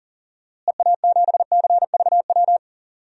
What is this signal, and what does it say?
Morse code, heard as a keyed tone, about 40 wpm, 700 Hz. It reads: EA7CVW
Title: DXCC-100 Call Signs - 6 Characters - Repeated 3x - 40wpm